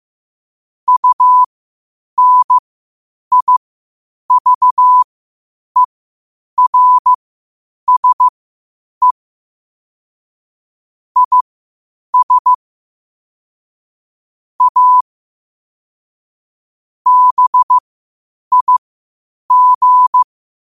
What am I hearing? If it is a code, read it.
Morse code, heard as a keyed tone, about 15 wpm, spaced as for 8 wpm, 1000 Hz. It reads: UNIVERSE IS A BIG
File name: qotdfiles / 8wpm